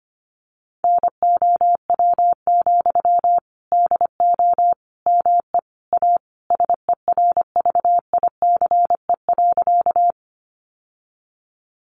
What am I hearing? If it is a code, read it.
Morse code, heard as a keyed tone, about 25 wpm, 700 Hz. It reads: NOW, DO ME A SERVICE.
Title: Sentences from Top 700 Words 25wpm